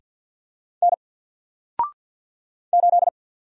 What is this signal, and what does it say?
Morse code, heard as a keyed tone, about 48 wpm, 700 Hz. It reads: N 8
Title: Single Letter-Number Alt 50wpm